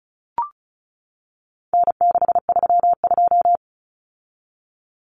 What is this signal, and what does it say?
Morse code, heard as a keyed tone, about 34 wpm, 700 Hz. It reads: N632
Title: Common Cut Numbers - 4-Digit 35wpm